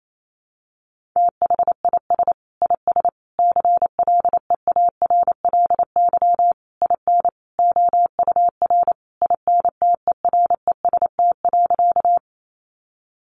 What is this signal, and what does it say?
Morse code, heard as a keyed tone, about 28 wpm, 700 Hz. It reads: THIS IS CLEARLY IN OUR INTEREST.
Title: Sentences from Top 1300 Words 28wpm